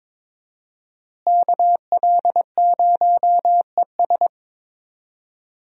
Morse code, heard as a keyed tone, about 22 wpm, 700 Hz. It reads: KL0ES
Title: USA Call Signs - 5 Characters - Repeated 2x - 22wpm